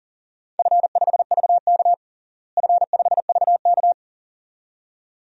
Morse code, heard as a keyed tone, about 40 wpm, 700 Hz. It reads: F5VK F5VK
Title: DXCC-100 Call Signs - 4 Characters - Repeated 2x - 40wpm